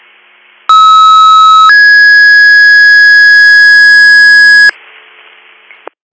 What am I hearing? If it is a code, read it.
Two-tone page, heard as a one-second tone, then a three-second tone, about 1290 then 1650 Hz. It is heard from 0.7 to 4.7 s.